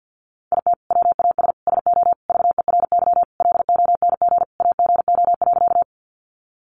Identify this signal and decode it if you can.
Morse code, heard as a keyed tone, about 50 wpm, 700 Hz. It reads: IT WAS SO VERY LONG AGO.
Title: Sentences from Top 500 Words 50wpm